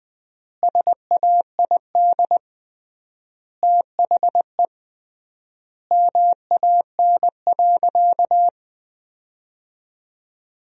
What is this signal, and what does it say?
Morse code, heard as a keyed tone, about 20 wpm, 700 Hz. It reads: SAID THE MAN.
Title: 3x Word-Spacing - Sentences from Top 400 Words - 20wpm